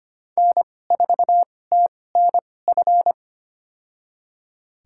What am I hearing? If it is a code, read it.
Morse code, heard as a keyed tone, about 25 wpm, spaced as for 18 wpm, 700 Hz. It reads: N4TNF
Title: USA Call Signs - 5 Characters - 2x Character Spacing - 25wpm